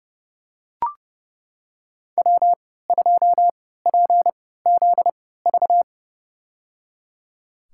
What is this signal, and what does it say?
Morse code, heard as a keyed tone, about 30 wpm, spaced as for 17 wpm, 700 Hz. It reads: W2PZV
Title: USA Call Signs - 5 Characters - 3x Character Spacing - 30wpm